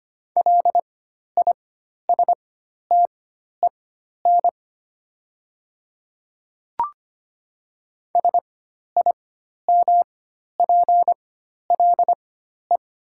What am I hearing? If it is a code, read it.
Morse code, heard as a keyed tone, about 25 wpm, spaced as for 12 wpm, 700 Hz. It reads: LISTEN SIMPLE
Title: 4x Character Spacing - Top 500 Words - 25wpm